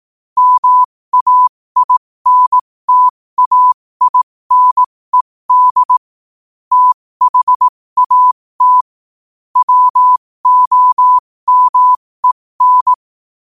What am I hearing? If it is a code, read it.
Morse code, heard as a keyed tone, about 18 wpm, spaced as for 15 wpm, 1000 Hz. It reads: MAINTAINED THAT WOMEN
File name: qotdfiles / 15wpm